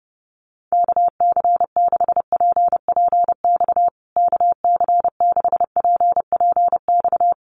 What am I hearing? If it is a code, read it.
Morse code, heard as a keyed tone, about 28 wpm, 700 Hz. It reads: KC6PPX KC6PPX
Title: USA Call Signs Repeated 3x 30wpm